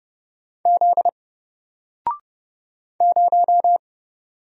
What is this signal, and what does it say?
Morse code, heard as a keyed tone, about 30 wpm, 700 Hz. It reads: Z 0
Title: Single Letter-Number 30wpm